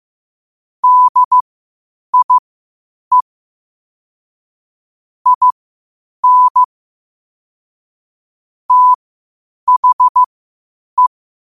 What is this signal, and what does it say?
Morse code, heard as a keyed tone, about 15 wpm, spaced as for 8 wpm, 1000 Hz. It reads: DIE IN THE
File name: qotdfiles / 8wpm